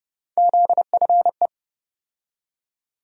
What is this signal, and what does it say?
Morse code, heard as a keyed tone, about 30 wpm, 700 Hz. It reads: ZFE
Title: USA Call Sign Postfixes - 3 Letters 30wpm